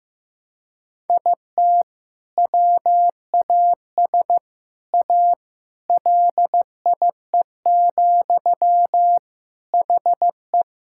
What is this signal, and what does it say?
Morse code, heard as a keyed tone, about 15 wpm, 700 Hz. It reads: IT WAS A LIE, HE
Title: Sentences from Top 1600 Words 15wpm